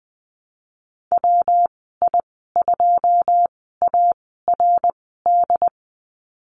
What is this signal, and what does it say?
Morse code, heard as a keyed tone, about 19 wpm, spaced as for 14 wpm, 700 Hz. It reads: WI2ARD
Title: 154 Lesson 25g - Letter p - Call Signs 20wpm